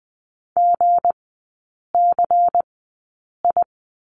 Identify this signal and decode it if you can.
Morse code, heard as a keyed tone, about 19 wpm, spaced as for 8 wpm, 690 Hz. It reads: GCI
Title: 212 Lesson 34e - Number 8 - Three Characters 20wpm